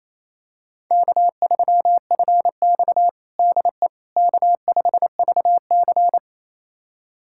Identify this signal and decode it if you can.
Morse code, heard as a keyed tone, about 28 wpm, 700 Hz. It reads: K3FX DE K5VC
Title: USA Callsign DE Callsign - 4 Characters - 28wpm